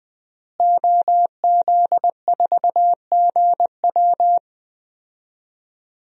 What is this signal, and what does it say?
Morse code, heard as a keyed tone, about 20 wpm, 700 Hz. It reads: OZ4GW
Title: DXCC-100 Call Signs - 5 Characters - Repeated 2x - 20wpm